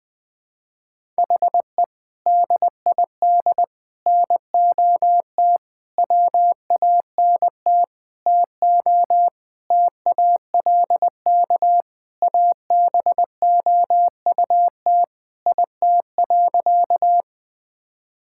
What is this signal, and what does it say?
Morse code, heard as a keyed tone, about 21 wpm, 700 Hz. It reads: HE DID NOT WANT TO TALK ABOUT IT.
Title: Sentences with 5-Word Phrases - Spoken Beginning - 20wpm